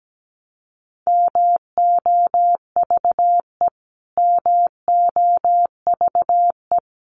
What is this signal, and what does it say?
Morse code, heard as a keyed tone, about 17 wpm, 700 Hz. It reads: MOVE MOVE
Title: Top 200 Words Repeated 17wpm